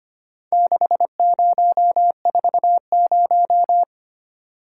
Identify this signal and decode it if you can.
Morse code, heard as a keyed tone, about 26 wpm, 690 Hz. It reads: 6040
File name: practice / 25wpm / Four Letter-Number Combinations - 25wpm